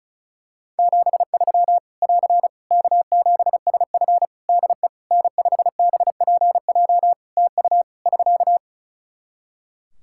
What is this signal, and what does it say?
Morse code, heard as a keyed tone, about 36 wpm, 700 Hz. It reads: 73 <AR> K7SF DE N5BPJ TU <SK>
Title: Practice QSO - Exchange 4 - Ending - 35wpm